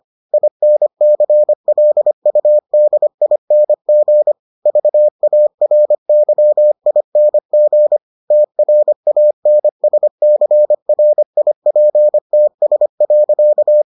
Morse code, heard as a keyed tone, about 25 wpm, 600 Hz. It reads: INCLUDING VARYING TRANSCRIPTS.